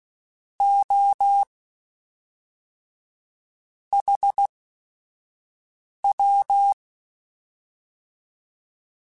Morse code, heard as a keyed tone, about 16 wpm, spaced as for 5 wpm, 800 Hz. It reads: O HW